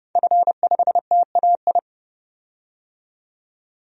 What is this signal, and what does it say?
Morse code, heard as a keyed tone, about 30 wpm, 700 Hz. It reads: F5TAI
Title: DXCC-100 Call Signs - 5 Characters - 30wpm